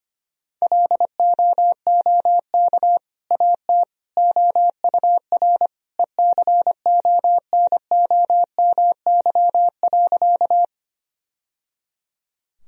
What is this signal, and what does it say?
Morse code, heard as a keyed tone, about 25 wpm, 700 Hz. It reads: LOOK AT OUR ECONOMY.
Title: Sentences from Top 1000 Words - Review Part 3 - 25wpm